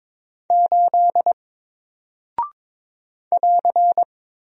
Morse code, heard as a keyed tone, about 22 wpm, 700 Hz. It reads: 8 <AR>
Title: Single Character (Letter:Number:Punctuation:Prosign) 22wpm